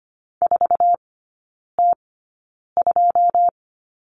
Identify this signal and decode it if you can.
Morse code, heard as a keyed tone, about 25 wpm, spaced as for 9 wpm, 700 Hz. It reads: 4T2
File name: Lesson 14e - Number 2 - Three Characters 25wpm